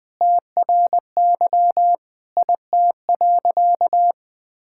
Morse code, heard as a keyed tone, about 20 wpm, 700 Hz. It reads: TRY IT.
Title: Sentences with 6-Word Phrases - 20wpm